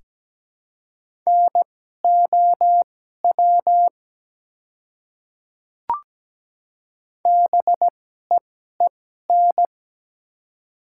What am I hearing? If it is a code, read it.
Morse code, heard as a keyed tone, about 17 wpm, spaced as for 12 wpm, 690 Hz. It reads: NOW BEEN
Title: 2x Character Spacing - Top 100 Words - 17wpm